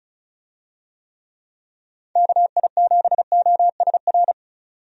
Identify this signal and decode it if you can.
Morse code, heard as a keyed tone, about 36 wpm, 700 Hz. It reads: KI7OSR
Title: USA Call Signs - 6 Characters - 35wpm